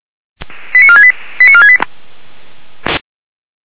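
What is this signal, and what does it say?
Five-tone selective call (ZVEI), heard as quick runs of five tones, tones of about 70 ms. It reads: 89468 89468
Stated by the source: ZVEI five-tone selective call received on an SDR.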